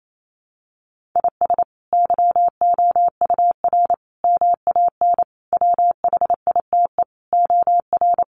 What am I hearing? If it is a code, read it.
Morse code, heard as a keyed tone, about 28 wpm, 700 Hz. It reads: IS YOUR MAN WHITE OR
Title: Sentences from Top 400 Words 28wpm